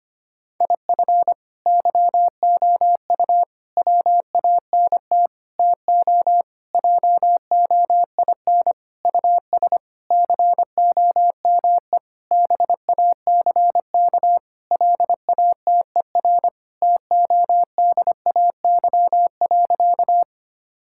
Morse code, heard as a keyed tone, about 25 wpm, 700 Hz. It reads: IF YOU WANT TO JOIN US COME BACK LATER TODAY.
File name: practice / 25wpm / Sentences with 4-Word Phrases - 25wpm